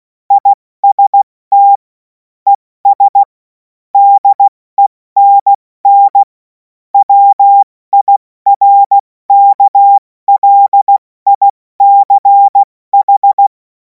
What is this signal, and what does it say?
Morse code, heard as a keyed tone, about 16 wpm, 800 Hz. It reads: IST ES DENN WIRKLICH
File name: Morsekurs / Witze / Witz 07. Juli 2026 70 F80